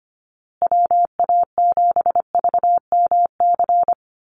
Morse code, heard as a keyed tone, about 25 wpm, 700 Hz. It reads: WA7VMC
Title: USA Call Signs Repeated 3x 25wpm